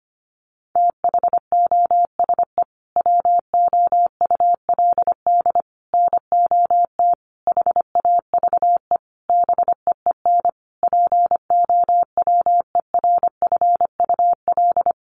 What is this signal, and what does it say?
Morse code, heard as a keyed tone, about 25 wpm, 700 Hz. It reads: THOSE WOULD NOT HAVE BEEN POWERFUL
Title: Sentences from Top 1300 Words 25wpm